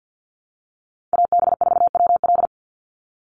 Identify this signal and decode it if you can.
Morse code, heard as a keyed tone, about 49 wpm, 700 Hz. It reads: AB4WL
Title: NA SOTA Chaser Callsigns 50wpm